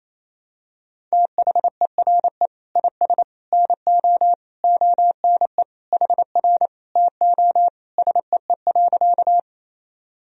Morse code, heard as a keyed tone, about 28 wpm, 700 Hz. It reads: THERE IS NO ONE HR TO SEE.
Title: Sentences from Top 200 Words and Abbreviations 28wpm